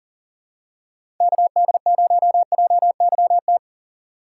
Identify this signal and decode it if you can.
Morse code, heard as a keyed tone, about 40 wpm, 700 Hz. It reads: KD0JYT
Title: USA Call Signs - 6 Characters - 3x Character Spacing - 40wpm